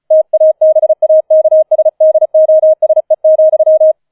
Morse code, heard as a keyed tone, about 36 wpm, 630 Hz. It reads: TABAKSDOSE,
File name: 004 Vierter Streich 35wpm